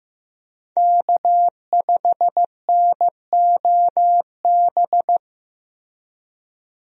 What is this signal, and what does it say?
Morse code, heard as a keyed tone, about 15 wpm, 700 Hz. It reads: K5NOB
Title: USA Call Signs - 5 Characters - Repeated 3x - 15wpm